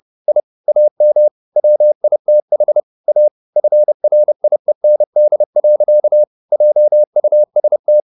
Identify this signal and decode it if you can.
Morse code, heard as a keyed tone, about 30 wpm, 600 Hz. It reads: I AM WITH A FRIEND. JUST
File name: the old man and the sea / the old man and the sea 1 30wpm